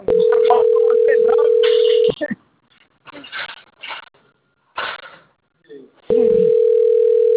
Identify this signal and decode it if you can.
Ringback tone (North America), 440 + 480 Hz.